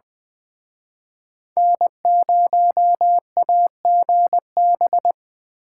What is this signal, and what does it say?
Morse code, heard as a keyed tone, about 23 wpm, 700 Hz. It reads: N0AGB
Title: Speed Racing - USA 1x3 Call Signs 20wpm